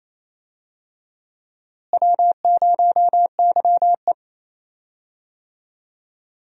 Morse code, heard as a keyed tone, about 28 wpm, 700 Hz. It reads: W0YE
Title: USA Call Signs - 4 Characters - Repeated 3x - 28wpm